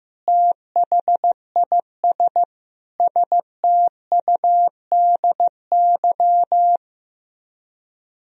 Morse code, heard as a keyed tone, about 15 wpm, 700 Hz. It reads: THIS STUDY
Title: Speed-Racing - Sets of 5 Words 15wpm